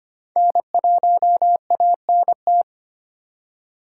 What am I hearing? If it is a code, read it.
Morse code, heard as a keyed tone, about 25 wpm, 700 Hz. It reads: N1ANT
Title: USA Call Signs - 5 Characters - 2x Character Spacing - 25wpm